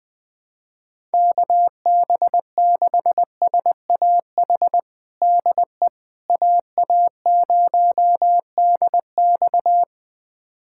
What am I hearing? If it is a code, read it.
Morse code, heard as a keyed tone, about 20 wpm, 700 Hz. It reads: KB6SAH DE AA0DX
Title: USA Callsign DE Callsign 20wpm